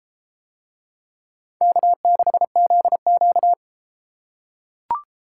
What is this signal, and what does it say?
Morse code, heard as a keyed tone, about 31 wpm, 700 Hz. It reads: K6ZQ